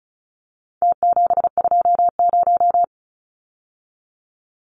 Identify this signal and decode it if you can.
Morse code, heard as a keyed tone, about 34 wpm, 700 Hz. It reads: T720
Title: Numbers - Contest Style - 4-Digit 35wpm